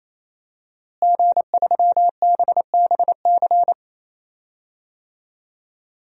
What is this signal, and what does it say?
Morse code, heard as a keyed tone, about 28 wpm, 700 Hz. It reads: G3BBC
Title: DXCC-100 Call Signs - 5 Characters - Repeated 3x - 28wpm